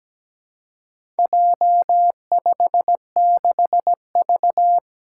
Morse code, heard as a keyed tone, about 17 wpm, 700 Hz. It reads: J56V
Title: Four Challenging Letter-Number Combinations 17wpm